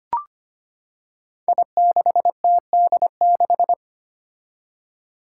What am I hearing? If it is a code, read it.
Morse code, heard as a keyed tone, about 25 wpm, 700 Hz. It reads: I6TD6
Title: Five Challenging Letter-Number Combinations 25wpm